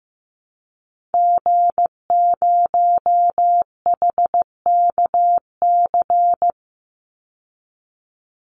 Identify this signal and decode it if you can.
Morse code, heard as a keyed tone, about 15 wpm, 700 Hz. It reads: G0HKC
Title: Call Signs Repeated 2x - 15wpm